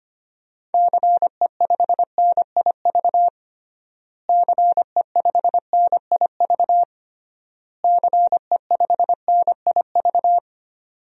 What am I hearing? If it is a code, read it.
Morse code, heard as a keyed tone, about 25 wpm, 700 Hz. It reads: CE5NIV CE5NIV CE5NIV
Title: DXCC-100 Call Signs - 6 Characters - Repeated 3x - 25wpm